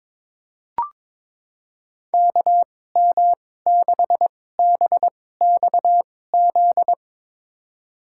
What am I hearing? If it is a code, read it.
Morse code, heard as a keyed tone, about 22 wpm, spaced as for 16 wpm, 700 Hz. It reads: KM6BXZ
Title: USA Call Signs - 6 Characters - 2x Character Spacing - 22wpm